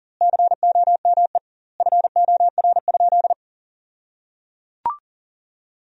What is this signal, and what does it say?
Morse code, heard as a keyed tone, about 37 wpm, 700 Hz. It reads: COME FOR?
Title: 2x Word-Spacing - Sentences from Top 100 Words - 40wpm